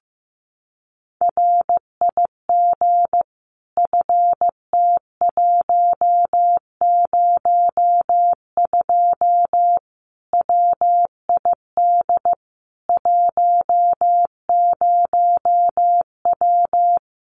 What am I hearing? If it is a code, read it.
Morse code, heard as a keyed tone, about 15 wpm, 700 Hz. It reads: RIG FT102 WID 10W
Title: Common QSO Phrases 15wpm